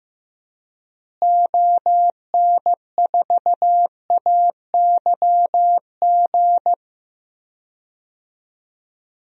Morse code, heard as a keyed tone, about 15 wpm, 700 Hz. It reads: ON4AYG
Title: DXCC-100 Call Signs - 6 Characters - Repeated 2x - 15wpm